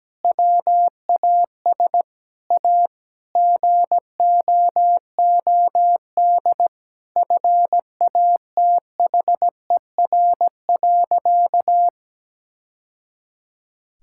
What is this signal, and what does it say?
Morse code, heard as a keyed tone, about 17 wpm, 700 Hz. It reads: WAS A GOOD FATHER.
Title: Sentences from Top 1000 Words - Review Part 5 - 17wpm